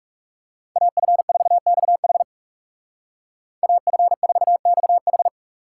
Morse code, heard as a keyed tone, about 35 wpm, 700 Hz. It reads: AF4XH AF4XH
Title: Speed Racing - USA 2x2 Call Signs 30wpm